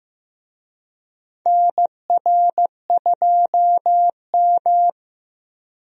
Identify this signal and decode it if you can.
Morse code, heard as a keyed tone, about 15 wpm, 700 Hz. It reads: NR2M
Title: USA Call Signs - 4 Characters - Repeated 3x - 15wpm